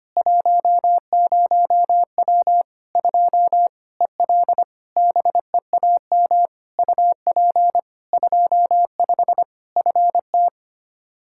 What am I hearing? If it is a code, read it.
Morse code, heard as a keyed tone, about 25 wpm, 700 Hz. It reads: 10W 2 EL BEAM UP 25 FT
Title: Practice QSO - Exchange 2 - Info - 25wpm